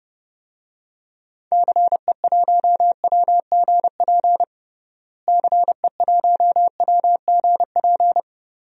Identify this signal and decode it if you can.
Morse code, heard as a keyed tone, about 29 wpm, 700 Hz. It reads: CE1WGP CE1WGP
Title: DXCC-100 Call Signs - 6 Characters - Repeated 3x - 30wpm